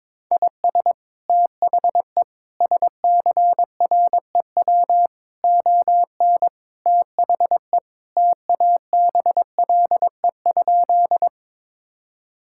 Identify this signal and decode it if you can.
Morse code, heard as a keyed tone, about 22 wpm, 700 Hz. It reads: IS THE SCREW ON THE TABLE?
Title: Sentences from Top 1600 Words 22wpm